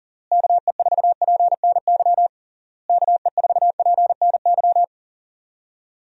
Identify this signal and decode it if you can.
Morse code, heard as a keyed tone, about 40 wpm, 700 Hz. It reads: KE4PNY KE4PNY
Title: USA Call Signs - 6 Characters - Repeated 3x - 40wpm